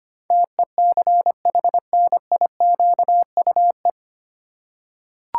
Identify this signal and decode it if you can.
Morse code, heard as a keyed tone, about 25 wpm, 700 Hz. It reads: TECHNIQUE
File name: Nine Letter Words - 25wpm